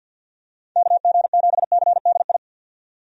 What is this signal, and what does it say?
Morse code, heard as a keyed tone, about 46 wpm, 700 Hz. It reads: KG7CDI
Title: USA Call Signs - 6 Characters - 50wpm